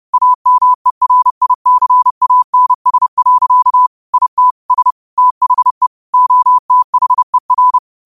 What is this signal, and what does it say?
Morse code, heard as a keyed tone, about 30 wpm, 1000 Hz. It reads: AMERICANS. IT S THE OTHER